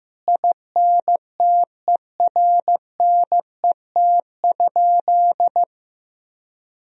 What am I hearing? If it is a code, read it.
Morse code, heard as a keyed tone, about 15 wpm, 700 Hz. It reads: INTERNET?
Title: Sentences from Top 1000 Words - Review Part 4 - 15wpm